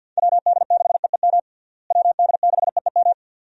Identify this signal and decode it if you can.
Morse code, heard as a keyed tone, about 52 wpm, 700 Hz. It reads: WD6EEM WD6EEM
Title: USA Call Signs - 6 Characters - Repeated 2x - 50wpm